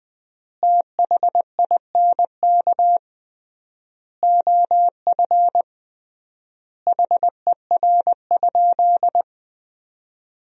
Morse code, heard as a keyed tone, about 21 wpm, 700 Hz. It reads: THINK OF HER?
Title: 3x Word-Spacing - Sentences from Top 200 Words - 20wpm